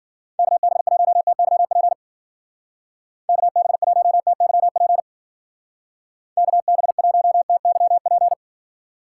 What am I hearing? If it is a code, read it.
Morse code, heard as a keyed tone, about 50 wpm, 700 Hz. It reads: KB1TYP KB1TYP KB1TYP
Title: Speed Racing - USA 2x3 Call Signs 40wpm